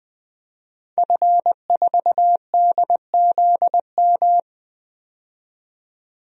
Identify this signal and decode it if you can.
Morse code, heard as a keyed tone, about 20 wpm, 700 Hz. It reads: F4DZM
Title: DXCC-100 Call Signs - 5 Characters - 20wpm